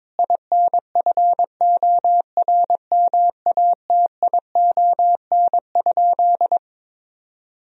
Morse code, heard as a keyed tone, about 21 wpm, 700 Hz. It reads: INFORMATION?
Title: Sentences from Top 1000 Words - Review Part 4 - 22wpm